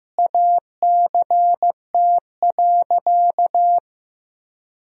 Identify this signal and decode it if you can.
Morse code, heard as a keyed tone, about 15 wpm, 700 Hz. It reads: ACT.
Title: Sentences from Top 1000 Words - Review Part 1 - 15wpm